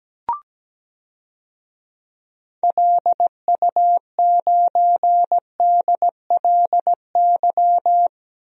Morse code, heard as a keyed tone, about 17 wpm, 700 Hz. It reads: LU9DLY
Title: DXCC-100 Call Signs - 6 Characters - 17wpm